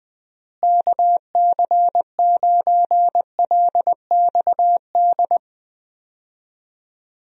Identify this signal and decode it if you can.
Morse code, heard as a keyed tone, about 20 wpm, 700 Hz. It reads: KC9LXD
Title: USA Call Signs - 6 Characters - Repeated 3x - 20wpm